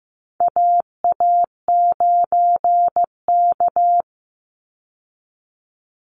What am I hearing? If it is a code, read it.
Morse code, heard as a keyed tone, about 15 wpm, 700 Hz. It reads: AA9K
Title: Call Signs Repeated 3x - 15wpm